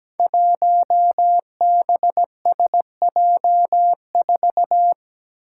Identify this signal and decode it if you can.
Morse code, heard as a keyed tone, about 17 wpm, 700 Hz. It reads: 1BSJ4
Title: Five Challenging Letter-Number Combinations 17wpm